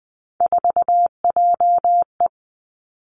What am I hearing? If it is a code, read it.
Morse code, heard as a keyed tone, about 20 wpm, 700 Hz. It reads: 4JE
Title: Three Challenging Letter-Number Combinations 20wpm